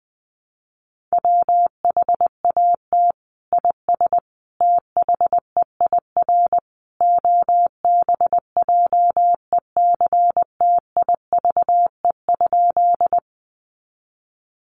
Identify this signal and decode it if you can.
Morse code, heard as a keyed tone, about 20 wpm, 700 Hz. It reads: WHAT IS THEIR OBJECTIVE?
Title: Sentences from Top 1200 Words 20wpm